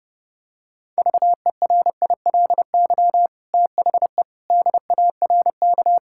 Morse code, heard as a keyed tone, about 30 wpm, 700 Hz. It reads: VERILY THE DARK
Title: Pangram Challenge v2 - 30wpm